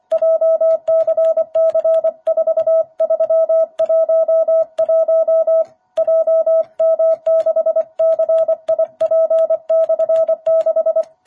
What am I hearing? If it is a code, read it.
Morse code, heard as a keyed tone, about 25 wpm, 660 Hz. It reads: JCC4311 JM6CIP/6
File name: jh8cbh / 2021cw